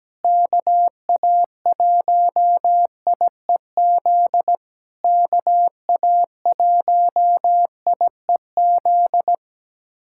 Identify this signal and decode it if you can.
Morse code, heard as a keyed tone, about 17 wpm, 700 Hz. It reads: KA1IEZ KA1IEZ
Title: SOTA QSO Practice 17wpm